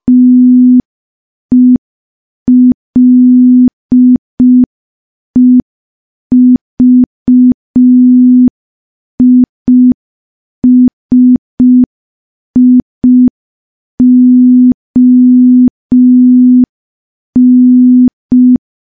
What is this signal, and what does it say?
Morse code, heard as a keyed tone, about 5 wpm, 250 Hz. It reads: TELEVISION
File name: morse / morse-training-05WPM